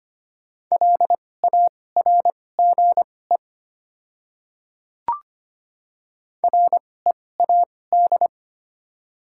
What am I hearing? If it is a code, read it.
Morse code, heard as a keyed tone, about 25 wpm, spaced as for 18 wpm, 700 Hz. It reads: LARGE READ
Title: 2x Character Spacing - Top 200 Words - 25wpm